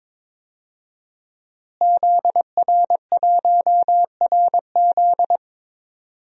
Morse code, heard as a keyed tone, about 22 wpm, 700 Hz. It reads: ZR1RZ
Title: DXCC-100 Call Signs - 5 Characters - Repeated 3x - 22wpm